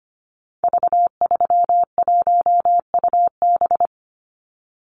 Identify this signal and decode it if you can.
Morse code, heard as a keyed tone, about 25 wpm, 700 Hz. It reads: V31UB
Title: Call Signs - 25wpm